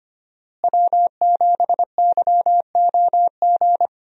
Morse code, heard as a keyed tone, about 25 wpm, 700 Hz. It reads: W7YOG
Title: USA Call Signs - 5 Characters - Repeated 3x - 25wpm